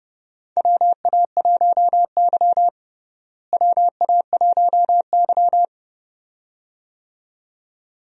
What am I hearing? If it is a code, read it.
Morse code, heard as a keyed tone, about 30 wpm, 700 Hz. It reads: WA1Y WA1Y
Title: USA Call Signs - 4 Characters - Repeated 2x - 30wpm